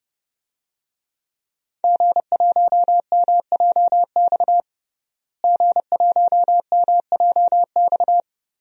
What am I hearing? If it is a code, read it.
Morse code, heard as a keyed tone, about 30 wpm, 700 Hz. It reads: G1MJX G1MJX
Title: DXCC-100 Call Signs - 5 Characters - Repeated 3x - 30wpm